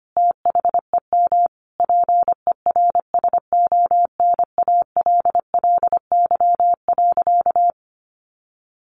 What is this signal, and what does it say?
Morse code, heard as a keyed tone, about 25 wpm, 700 Hz. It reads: THEM PERSONALLY.
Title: Sentences from Top 1400 Words 25wpm